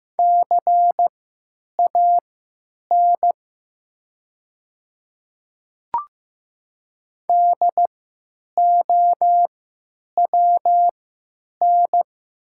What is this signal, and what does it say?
Morse code, heard as a keyed tone, about 15 wpm, spaced as for 9 wpm, 700 Hz. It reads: CAN DOWN
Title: 3x Character Spacing - Top 500 Words - 15wpm